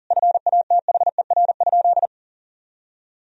Morse code, heard as a keyed tone, about 37 wpm, 700 Hz. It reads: FATHER?
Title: Sentences from Top 1000 Words - Review Part 3 - 40wpm